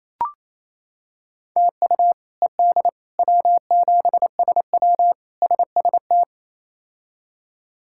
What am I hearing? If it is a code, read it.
Morse code, heard as a keyed tone, about 27 wpm, 700 Hz. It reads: TU ED W7SW SST
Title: K1USN SST QSO Practice - 28wpm